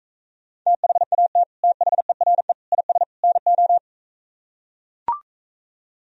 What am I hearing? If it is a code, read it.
Morse code, heard as a keyed tone, about 44 wpm, 700 Hz. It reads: THAT THERE IS NO